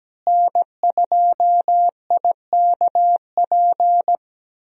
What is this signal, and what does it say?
Morse code, heard as a keyed tone, about 17 wpm, 700 Hz. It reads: N2IKP
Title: USA Call Signs - 5 Characters - Repeated 3x - 17wpm